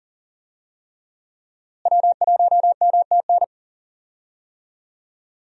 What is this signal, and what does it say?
Morse code, heard as a keyed tone, about 40 wpm, 700 Hz. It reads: W1MTN
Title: USA Call Signs - 5 Characters - Repeated 3x - 40wpm